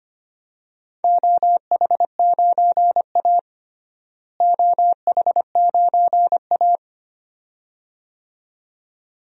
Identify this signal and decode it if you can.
Morse code, heard as a keyed tone, about 25 wpm, 700 Hz. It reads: OH9A OH9A
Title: DXCC-100 Call Signs - 4 Characters - Repeated 2x - 25wpm